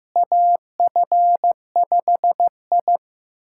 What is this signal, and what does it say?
Morse code, heard as a keyed tone, about 15 wpm, 700 Hz. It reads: AF5I
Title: USA Callsign DE Callsign - 4 Characters - 15wpm